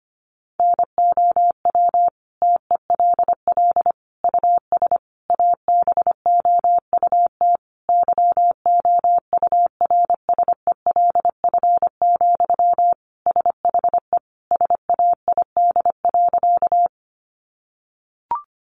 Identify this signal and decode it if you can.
Morse code, heard as a keyed tone, about 25 wpm, 700 Hz. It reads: NOW TELL US ABOUT YOURSELF, SHE SAID.
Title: Sentences from Top 600 Words 25wpm